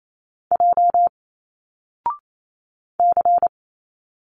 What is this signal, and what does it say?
Morse code, heard as a keyed tone, about 28 wpm, 700 Hz. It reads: J C
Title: Single Letters 28wpm